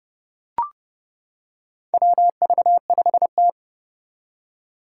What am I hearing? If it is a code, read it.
Morse code, heard as a keyed tone, about 29 wpm, 690 Hz. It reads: WV5T
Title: Four Challenging Letter-Number Combinations 30wpm